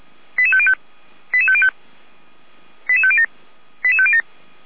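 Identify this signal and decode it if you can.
Five-tone selective call (ZVEI), heard as quick runs of five tones, tones of about 80 ms. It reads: 88505 88505 88507 88507
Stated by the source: ZVEI five-tone call received on SDR.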